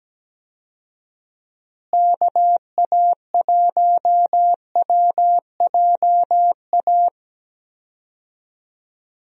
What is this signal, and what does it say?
Morse code, heard as a keyed tone, about 17 wpm, 700 Hz. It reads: KA1WJA
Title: USA Call Signs - 6 Characters - Repeated 3x - 17wpm